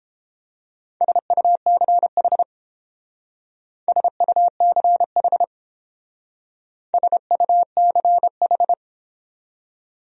Morse code, heard as a keyed tone, about 30 wpm, 700 Hz. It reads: SUCH SUCH SUCH